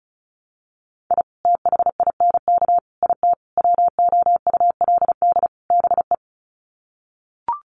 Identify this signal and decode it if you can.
Morse code, heard as a keyed tone, about 34 wpm, 700 Hz. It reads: I THINK IT WOULD BE
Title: Sets of 5 Words 35wpm